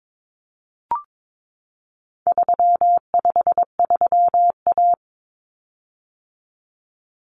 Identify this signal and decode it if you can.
Morse code, heard as a keyed tone, about 22 wpm, 700 Hz. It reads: 353A
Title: Common Cut Numbers - 4-Digit 22wpm